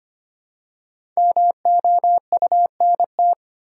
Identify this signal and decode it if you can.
Morse code, heard as a keyed tone, about 25 wpm, 700 Hz. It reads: MOUNT
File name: 5 Letter Words - First 3 Letters Given - 25wpm